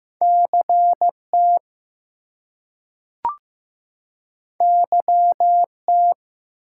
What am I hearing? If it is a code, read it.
Morse code, heard as a keyed tone, about 15 wpm, 700 Hz. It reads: CT YT
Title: US States and Canadian Provinces 15wpm